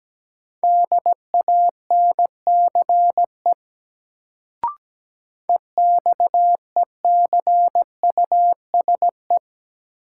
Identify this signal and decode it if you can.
Morse code, heard as a keyed tone, about 17 wpm, 700 Hz. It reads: DANCE EXECUSE
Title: Sentences from Top 700 Words - New Word Drill - 17wpm